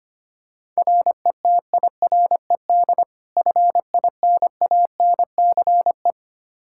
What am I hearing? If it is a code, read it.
Morse code, heard as a keyed tone, about 25 wpm, 700 Hz. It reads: RETIRED FINANCE
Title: K1USN SST QSO Practice 25wpm